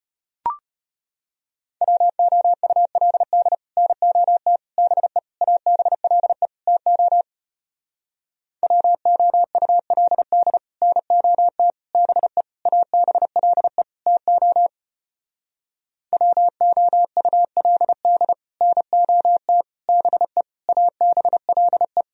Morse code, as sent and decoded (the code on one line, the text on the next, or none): .-- --- ..- .-.. -.. / -. --- - / -... / .- -... .-.. / - --- / .-- --- ..- .-.. -.. / -. --- - / -... / .- -... .-.. / - --- / .-- --- ..- .-.. -.. / -. --- - / -... / .- -... .-.. .
WOULD NOT BE ABLE TO WOULD NOT BE ABLE TO WOULD NOT BE ABLE